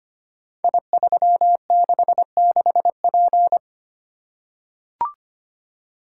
Morse code, heard as a keyed tone, about 25 wpm, 700 Hz. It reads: I366P
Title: Code Groups - 5 Letter:Numbers - 25wpm